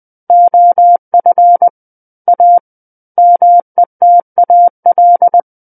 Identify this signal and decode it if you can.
Morse code, heard as a keyed tone, about 20 wpm, 700 Hz. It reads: OF A METAL
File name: 20 WPM